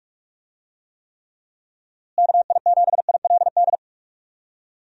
Morse code, heard as a keyed tone, about 46 wpm, 700 Hz. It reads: KI7ILD
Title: USA Call Signs - 6 Characters - Repeated 3x - 45wpm